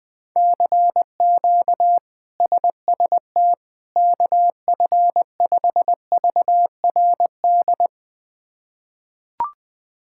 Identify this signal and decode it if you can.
Morse code, heard as a keyed tone, about 21 wpm, 700 Hz. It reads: CQ SST KF5VRD
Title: K1USN SST QSO Practice - 20wpm